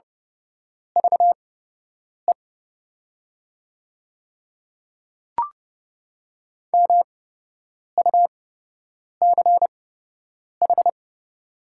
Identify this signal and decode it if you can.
Morse code, heard as a keyed tone, about 32 wpm, spaced as for 8 wpm, 700 Hz. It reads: VE MUCH
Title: 8x Character Spacing - Top 300 Words - 30wpm